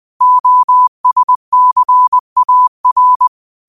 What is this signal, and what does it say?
Morse code, heard as a keyed tone, about 20 wpm, 1000 Hz. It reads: OSCAR
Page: New quotes every day in morse code at 20 Words per minute.